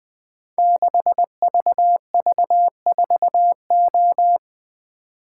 Morse code, heard as a keyed tone, about 20 wpm, 700 Hz. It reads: 6VV4O
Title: Code Groups - 5 Letter:Numbers - 20wpm